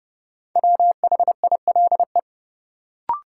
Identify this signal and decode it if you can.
Morse code, heard as a keyed tone, about 30 wpm, 700 Hz. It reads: WHILE
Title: Sets of 3 Words - New Word Drill - 30wpm